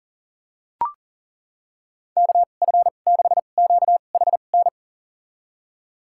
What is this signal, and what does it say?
Morse code, heard as a keyed tone, about 40 wpm, spaced as for 29 wpm, 700 Hz. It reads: KF6QHN
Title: USA Call Signs - 2x Character Spacing - 40wpm